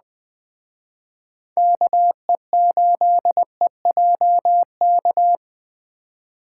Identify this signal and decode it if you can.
Morse code, heard as a keyed tone, about 21 wpm, 700 Hz. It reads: KE8EJK